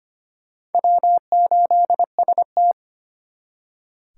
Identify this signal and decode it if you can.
Morse code, heard as a keyed tone, about 33 wpm, 700 Hz. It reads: W8ST